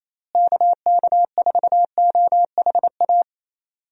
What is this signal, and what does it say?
Morse code, heard as a keyed tone, about 27 wpm, 700 Hz. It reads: KK4OHA
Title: USA Call Signs - 6 Characters - Repeated 3x - 28wpm